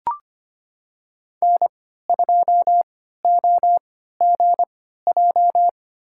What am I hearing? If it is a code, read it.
Morse code, heard as a keyed tone, about 25 wpm, spaced as for 14 wpm, 700 Hz. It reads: N2OGJ
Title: USA Call Signs - 3x Character Spacing - 25wpm